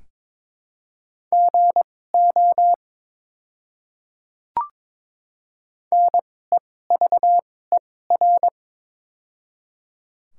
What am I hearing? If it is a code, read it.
Morse code, heard as a keyed tone, about 22 wpm, spaced as for 16 wpm, 700 Hz. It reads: GO NEVER
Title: 2x Character Spacing - Top 300 Words - 22wpm